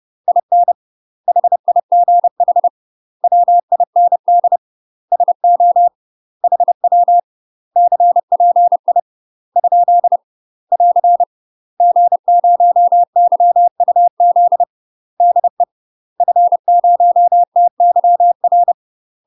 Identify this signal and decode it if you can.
Morse code, heard as a keyed tone, about 30 wpm, 700 Hz. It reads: IN HIGH WIND SO HW CPI ? <AR> G0YUZ DE F0TYR
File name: UK MASTER QSOs TRAINING / 30WPM / UK MASTER QSOs 0004